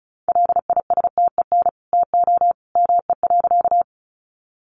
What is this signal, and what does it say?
Morse code, heard as a keyed tone, about 34 wpm, 700 Hz. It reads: LISTEN TO ME.
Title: Sentences from Top 400 Words 35wpm